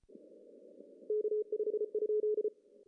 Morse code, heard as a keyed tone, about 34 wpm, 410 Hz. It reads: K5?
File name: CW / cqww-cw-k5tr